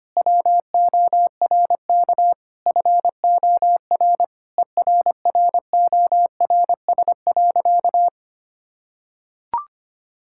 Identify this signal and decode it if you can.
Morse code, heard as a keyed tone, about 25 wpm, 700 Hz. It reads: WORK FOR ERRORS.
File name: practice / 25wpm / Difficult Sentences with Hints - 25wpm